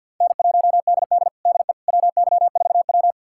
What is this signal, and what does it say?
Morse code, heard as a keyed tone, about 48 wpm, 700 Hz. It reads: N1DD DE WY4W
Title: USA Callsign DE Callsign - 4 Characters - 50wpm